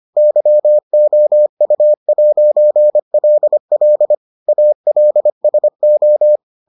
Morse code, heard as a keyed tone, about 25 wpm, 600 Hz. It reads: YOU'LL ALSO